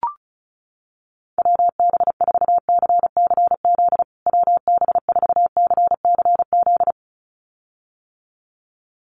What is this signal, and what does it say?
Morse code, heard as a keyed tone, about 34 wpm, 700 Hz. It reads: WB4CCZ WB4CCZ
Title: USA Call Signs Repeated 2x 35wpm